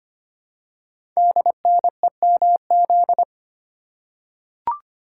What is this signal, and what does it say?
Morse code, heard as a keyed tone, about 25 wpm, 700 Hz. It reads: DNEMZ
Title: Code Groups - 5 Letters - 25wpm